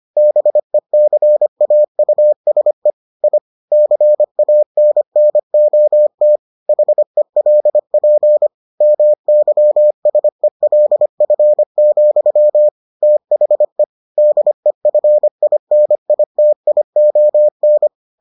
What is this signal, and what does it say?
Morse code, heard as a keyed tone, about 25 wpm, 600 Hz. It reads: BECAUSE I CANNOT HELP MYSELF, THE DEFINITION